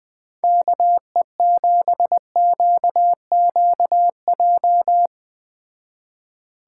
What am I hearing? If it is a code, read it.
Morse code, heard as a keyed tone, about 19 wpm, 700 Hz. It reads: KE7QQJ
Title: Speed Racing - USA 2x3 Call Signs 17wpm